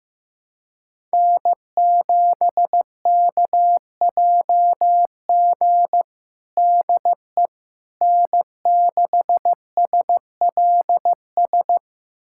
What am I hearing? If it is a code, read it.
Morse code, heard as a keyed tone, about 16 wpm, 700 Hz. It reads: N7KJG DE N6SLS
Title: USA Callsign DE Callsign - 5 Characters - 15wpm